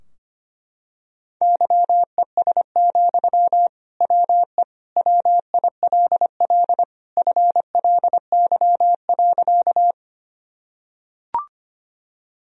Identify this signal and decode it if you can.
Morse code, heard as a keyed tone, about 25 wpm, 700 Hz. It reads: YES, WE WILL FLY.
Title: Sentences from Top 1000 Words - Review Part 2 - 25wpm